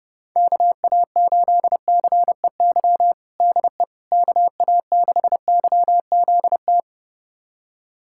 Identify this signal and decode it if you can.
Morse code, heard as a keyed tone, about 32 wpm, 700 Hz. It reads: KA8CEY DE KA6YZT
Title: USA Callsign DE Callsign - 6 Characters - 30wpm